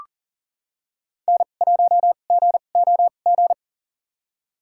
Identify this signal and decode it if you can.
Morse code, heard as a keyed tone, about 39 wpm, spaced as for 29 wpm, 700 Hz. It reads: N1GOG
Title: USA Call Signs - 5 Characters - 2x Character Spacing - 40wpm